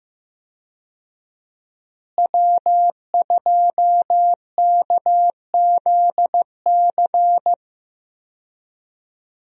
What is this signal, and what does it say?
Morse code, heard as a keyed tone, about 14 wpm, 700 Hz. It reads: W2KZC
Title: USA Call Signs - 5 Characters - Repeated 3x - 15wpm